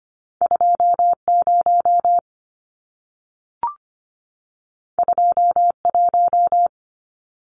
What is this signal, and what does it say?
Morse code, heard as a keyed tone, about 25 wpm, 700 Hz. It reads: 20 21
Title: Numbers - 2-Digit 25wpm